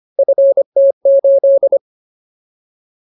Morse code, heard as a keyed tone, about 25 wpm, 550 Hz. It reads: FT8
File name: Ultimate Word List / 25WPM Ultimate